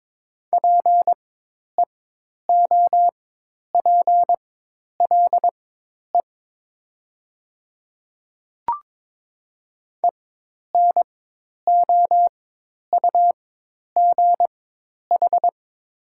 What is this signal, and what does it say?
Morse code, heard as a keyed tone, about 22 wpm, spaced as for 10 wpm, 700 Hz. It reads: PEOPLE ENOUGH
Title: 4x Character Spacing - Top 500 Words - 22wpm